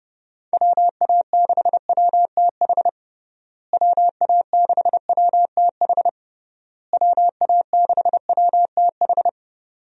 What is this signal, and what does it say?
Morse code, heard as a keyed tone, about 32 wpm, 700 Hz. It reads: WA6WTH WA6WTH WA6WTH
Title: USA Call Signs - 6 Characters - Repeated 3x - 30wpm